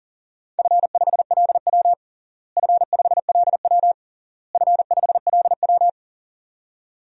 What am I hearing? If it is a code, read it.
Morse code, heard as a keyed tone, about 40 wpm, 700 Hz. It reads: F5LW F5LW F5LW
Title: DXCC-100 Call Signs - 4 Characters - Repeated 3x - 40wpm